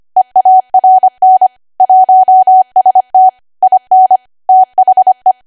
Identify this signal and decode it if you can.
Morse code, heard as a keyed tone, about 25 wpm, 750 Hz. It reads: EARN 1ST IN THE